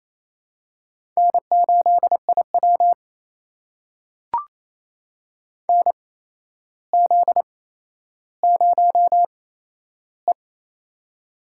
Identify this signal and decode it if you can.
Morse code, heard as a keyed tone, about 28 wpm, spaced as for 12 wpm, 700 Hz. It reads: N8IW NZ0E
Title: USA Call Signs - 4 Characters - 8x Character Spacing - 28wpm